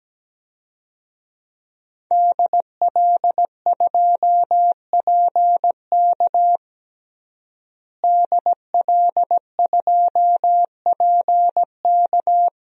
Morse code, heard as a keyed tone, about 17 wpm, 700 Hz. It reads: DL2PK DL2PK
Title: DXCC-100 Call Signs - 5 Characters - Repeated 3x - 17wpm